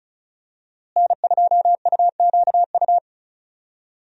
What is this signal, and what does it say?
Morse code, heard as a keyed tone, about 35 wpm, 700 Hz. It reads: N2UQU
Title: USA Call Signs - 5 Characters - Repeated 2x - 35wpm